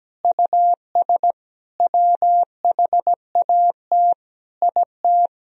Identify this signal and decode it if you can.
Morse code, heard as a keyed tone, about 17 wpm, 700 Hz. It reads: US WHAT IT
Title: Sentences from Top 400 Words and Abbreviations 17wpm